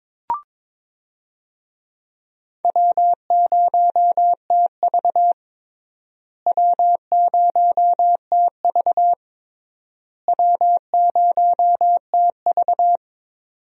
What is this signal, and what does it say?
Morse code, heard as a keyed tone, about 22 wpm, 700 Hz. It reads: W0TV W0TV W0TV
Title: USA Call Signs - 4 Characters - Repeated 3x - 22wpm